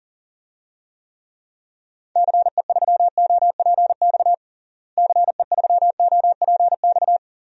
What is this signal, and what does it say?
Morse code, heard as a keyed tone, about 40 wpm, 700 Hz. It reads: CE3OPX CE3OPX
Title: DXCC-100 Call Signs - 6 Characters - Repeated 2x - 40wpm